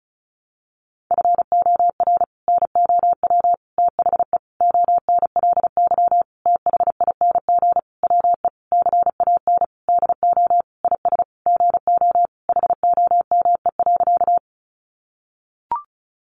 Morse code, heard as a keyed tone, about 34 wpm, 700 Hz. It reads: FOR NOW THE ONLY THING WE CAN DO IS GO HOME.
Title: Sentences from Top 200 Words 35wpm